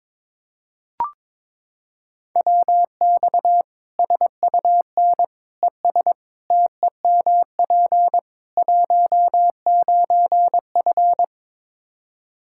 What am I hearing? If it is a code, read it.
Morse code, heard as a keyed tone, about 22 wpm, 700 Hz. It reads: WX SUN ES TEMP 19F
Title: Practice QSO - Exchange 2 - Info - 22wpm